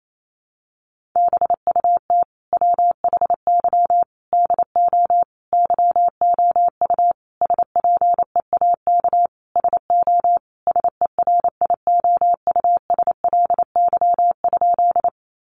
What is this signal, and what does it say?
Morse code, heard as a keyed tone, about 28 wpm, 700 Hz. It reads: BUT WHY DO YOU SPEAK SO SERIOUSLY?
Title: Sentences from Top 1200 Words 28wpm